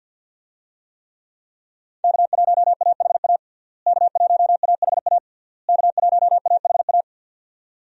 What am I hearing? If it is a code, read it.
Morse code, heard as a keyed tone, about 47 wpm, 700 Hz. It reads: K1AHA K1AHA K1AHA
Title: USA Call Signs - 5 Characters - Repeated 3x - 50wpm